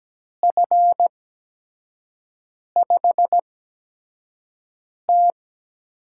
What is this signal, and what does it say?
Morse code, heard as a keyed tone, about 17 wpm, spaced as for 5 wpm, 700 Hz. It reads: F5T
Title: USA Call Signs - 5 Characters - 8x Character Spacing - 17wpm